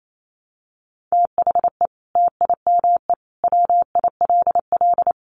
Morse code, heard as a keyed tone, about 28 wpm, 700 Hz. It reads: THE TIME WILL
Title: Sentences from Top 100 Words 28wpm